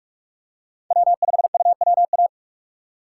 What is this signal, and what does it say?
Morse code, heard as a keyed tone, about 46 wpm, 700 Hz. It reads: W5UWA
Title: USA Call Signs - 2x Character Spacing - 45wpm